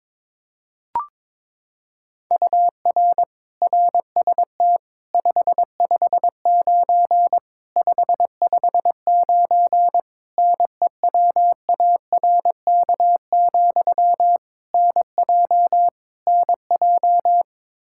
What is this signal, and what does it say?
Morse code, heard as a keyed tone, about 21 wpm, 700 Hz. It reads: UR RST 559 559 NEWARK, NJ NJ
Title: Practice QSO - Exchange 1 - Beginning - 22wpm